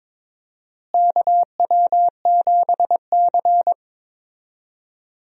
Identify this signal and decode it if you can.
Morse code, heard as a keyed tone, about 22 wpm, 700 Hz. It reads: KW7C
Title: USA Call Signs - 4 Characters - Repeated 2x - 22wpm